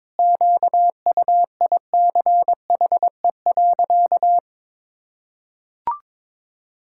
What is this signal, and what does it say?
Morse code, heard as a keyed tone, about 22 wpm, 700 Hz. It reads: QUICHE.
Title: Pangram Challenge v2 - 22wpm